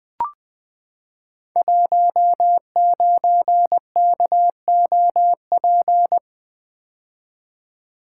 Morse code, heard as a keyed tone, about 20 wpm, 700 Hz. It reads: 19KOP
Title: Code Groups - 5 Letter:Numbers - 20wpm